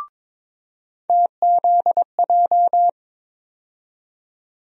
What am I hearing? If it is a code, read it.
Morse code, heard as a keyed tone, about 21 wpm, 700 Hz. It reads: TZJ
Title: USA Call Sign Postfixes - 3 Letters 22wpm